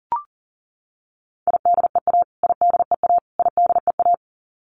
Morse code, heard as a keyed tone, about 40 wpm, 700 Hz. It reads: IDEA IDEA IDEA
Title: Top 500 Words Repeated 40wpm